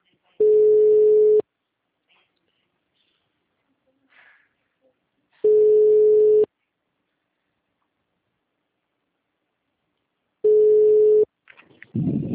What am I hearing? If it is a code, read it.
Ringback tone (Europe, 425 Hz).